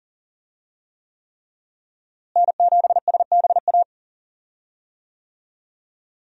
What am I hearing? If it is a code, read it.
Morse code, heard as a keyed tone, about 40 wpm, 700 Hz. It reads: N7SBA
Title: USA Call Signs - 5 Characters - Repeated 3x - 40wpm